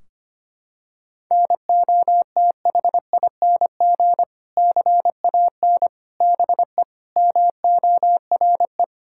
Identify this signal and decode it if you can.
Morse code, heard as a keyed tone, about 25 wpm, 700 Hz. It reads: NOTHING CAN BE MORE
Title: Sentences from Top 1600 Words 25wpm